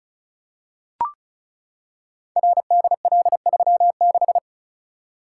Morse code, heard as a keyed tone, about 36 wpm, 700 Hz. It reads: RDL36
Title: Code Groups - 5 Letter:Numbers - 35wpm